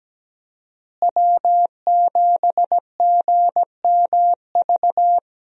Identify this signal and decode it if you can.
Morse code, heard as a keyed tone, about 17 wpm, 700 Hz. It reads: W7GMV
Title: USA Call Signs - 5 Characters - Repeated 2x - 17wpm